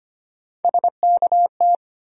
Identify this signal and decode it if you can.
Morse code, heard as a keyed tone, about 25 wpm, 700 Hz. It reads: SKT
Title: USA Call Sign Postfixes - 3 Letters 25wpm